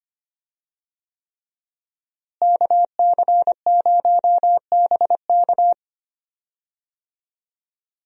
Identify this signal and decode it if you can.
Morse code, heard as a keyed tone, about 25 wpm, 700 Hz. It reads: KC0BK
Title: USA Call Signs - 5 Characters - Repeated 3x - 25wpm